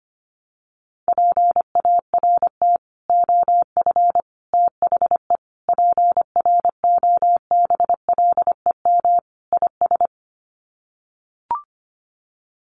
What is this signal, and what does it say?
Morse code, heard as a keyed tone, about 25 wpm, 700 Hz. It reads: PART OF THE PROBLEM IS
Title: Sets of 5 Words Encore2 25wpm